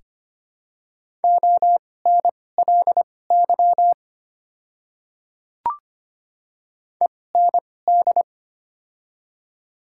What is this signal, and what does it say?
Morse code, heard as a keyed tone, about 25 wpm, spaced as for 18 wpm, 700 Hz. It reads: ONLY END
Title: 2x Character Spacing - Top 200 Words - 25wpm